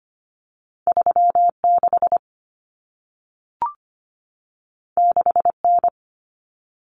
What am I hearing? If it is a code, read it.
Morse code, heard as a keyed tone, about 25 wpm, 700 Hz. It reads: 36 6N
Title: Common Cut Numbers - 2-Digit 25wpm